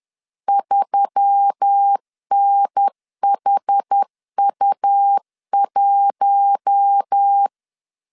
Morse code, heard as a keyed tone, about 10 wpm, 800 Hz. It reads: 3NHU1
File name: LEZIONI / 6 PKX0 / 6 PKX 10wpm 2426